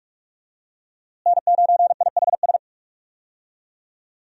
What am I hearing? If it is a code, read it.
Morse code, heard as a keyed tone, about 46 wpm, 700 Hz. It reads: N9IHS
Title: USA Call Signs - 5 Characters - Repeated 3x - 45wpm